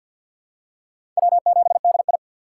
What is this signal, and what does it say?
Morse code, heard as a keyed tone, about 47 wpm, 700 Hz. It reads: W7DI
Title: USA Call Signs - 4 Characters - 3x Character Spacing - 50wpm